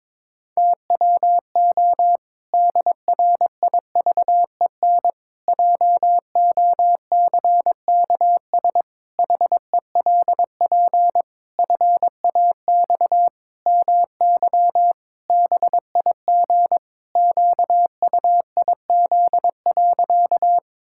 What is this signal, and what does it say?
Morse code, heard as a keyed tone, about 22 wpm, 700 Hz. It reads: TWO DRIVEN JOCKS HELP FAX MY BIG QUIZ.
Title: Pangram Challenge v2 - 22wpm